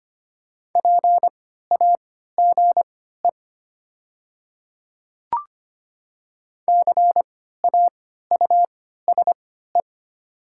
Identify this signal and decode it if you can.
Morse code, heard as a keyed tone, about 25 wpm, spaced as for 14 wpm, 700 Hz. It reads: PAGE CAUSE
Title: 3x Character Spacing - Top 500 Words - 25wpm